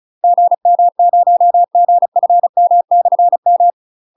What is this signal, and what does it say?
Morse code, heard as a keyed tone, about 35 wpm, 700 Hz. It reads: GM0GFM/M
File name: UK MASTER QSOs TRAINING / 35WPM / UK MASTER QSOs 0005